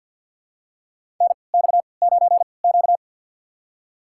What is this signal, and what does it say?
Morse code, heard as a keyed tone, about 52 wpm, spaced as for 29 wpm, 690 Hz. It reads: NX9Q
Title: USA Call Signs - 3x Character Spacing - 50wpm